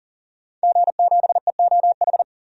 Morse code, heard as a keyed tone, about 40 wpm, 700 Hz. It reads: G7EOH
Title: DXCC-100 Call Signs - 5 Characters - 40wpm